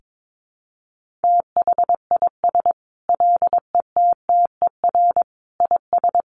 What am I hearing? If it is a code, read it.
Morse code, heard as a keyed tone, about 22 wpm, 700 Hz. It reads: THIS LETTER IS
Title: Sentences from Top 1400 Words 22wpm